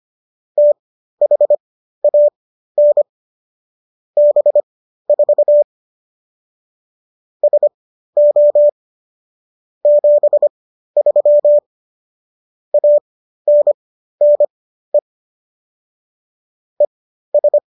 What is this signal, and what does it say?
Morse code, heard as a keyed tone, about 25 wpm, spaced as for 13 wpm, 590 Hz. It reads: THAN B4 SO 73 ANNE ES